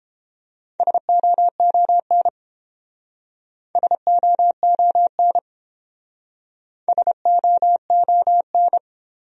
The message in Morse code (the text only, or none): SOON SOON SOON